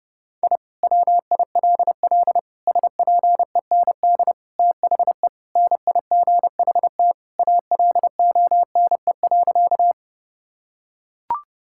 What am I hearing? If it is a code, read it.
Morse code, heard as a keyed tone, about 30 wpm, 700 Hz. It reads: I WILL SPEND THE NIGHT ALONE.
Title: Sentences from Top 1000 Words - Review Part 5 - 30wpm